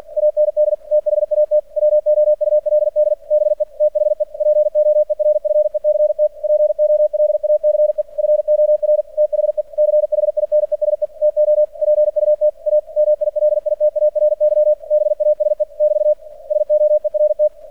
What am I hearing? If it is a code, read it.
Morse code, heard as a keyed tone, about 49 wpm, 600 Hz. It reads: AND THAT WOULD BE THE POERFECT POLACE FOR THE CHINESE TO PUT A MILITARY BASE <BT> SOERT
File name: archived / CRSnet